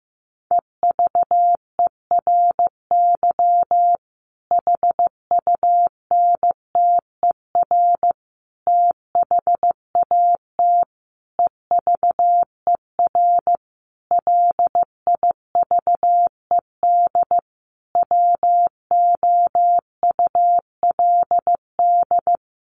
Morse code, heard as a keyed tone, about 15 wpm, 700 Hz. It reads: EVERY HUNTER THAT EVER LIVED WOULD
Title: A Sound Of Thunder 15wpm